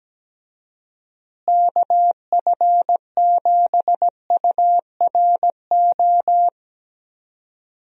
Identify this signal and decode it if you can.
Morse code, heard as a keyed tone, about 17 wpm, 700 Hz. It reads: KF7URO
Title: USA Call Signs - 6 Characters - Repeated 2x - 17wpm